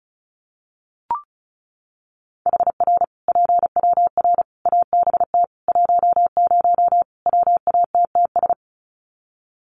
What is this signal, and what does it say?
Morse code, heard as a keyed tone, about 34 wpm, 700 Hz. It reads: HR PWR ABT 10 WATTS
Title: Common QSO Phrases 35wpm